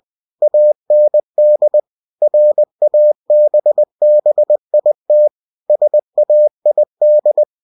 Morse code, heard as a keyed tone, about 20 wpm, 600 Hz. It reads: AND RABBIT SAID
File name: cw-books / winnie-the-pooh / winnie-20wpm-page-0084